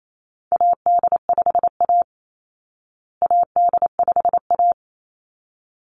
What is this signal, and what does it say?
Morse code, heard as a keyed tone, about 28 wpm, 700 Hz. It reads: AD5A AD5A
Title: Call Signs Repeated 3x - 28wpm